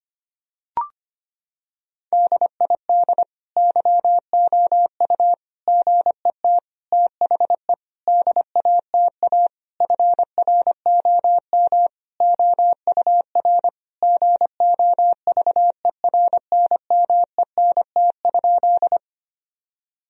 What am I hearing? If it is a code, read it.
Morse code, heard as a keyed tone, about 25 wpm, 700 Hz. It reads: DID YOU GET THE DATA FROM OUR GOVERNMENT?
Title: Sentences from Top 1000 Words - Review Part 3 - 25wpm